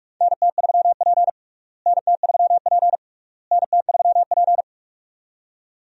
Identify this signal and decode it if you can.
Morse code, heard as a keyed tone, about 42 wpm, 700 Hz. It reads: NT3P NT3P NT3P
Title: USA Call Signs - 4 Characters - Repeated 3x - 45wpm